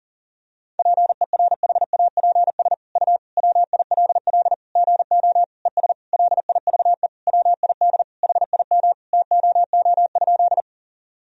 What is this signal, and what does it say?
Morse code, heard as a keyed tone, about 40 wpm, 700 Hz. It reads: PERHAPS U WILL GO ES LIVE WID HIM TOO?
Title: Sentences from Top 500 Words and Abbreviations 40wpm